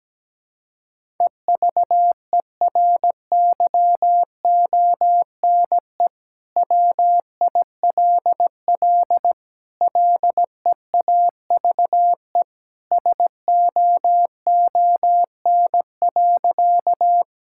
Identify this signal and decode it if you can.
Morse code, heard as a keyed tone, about 17 wpm, 700 Hz. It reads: EVERYONE WILL LEAVE SOON.
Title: Sentences from Top 1000 Words - Review Part 5 - 17wpm